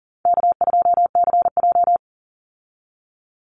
Morse code, heard as a keyed tone, about 40 wpm, 700 Hz. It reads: K2CJ
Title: Easy Call Signs Repeated 3x - 40wpm